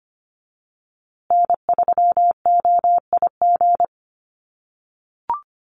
Morse code, heard as a keyed tone, about 25 wpm, 700 Hz. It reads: N3OIG
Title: USA Call Signs Repeated 2x 25wpm